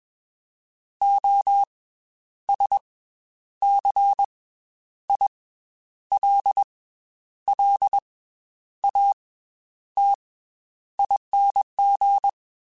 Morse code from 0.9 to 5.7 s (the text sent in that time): O S C I